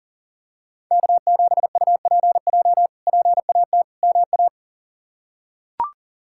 Morse code, heard as a keyed tone, about 37 wpm, 700 Hz. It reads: K7UPJ PAT MA
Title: K1USN SST QSO Practice - 40wpm